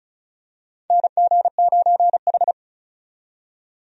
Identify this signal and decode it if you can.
Morse code, heard as a keyed tone, about 36 wpm, 700 Hz. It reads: NG9H
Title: USA Call Signs - 4 Characters - 35wpm